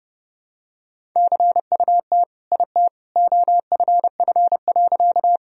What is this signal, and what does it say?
Morse code, heard as a keyed tone, about 32 wpm, 700 Hz. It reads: CUT IT OFF.
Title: Sentences from Top 1000 Words - Review Part 1 - 30wpm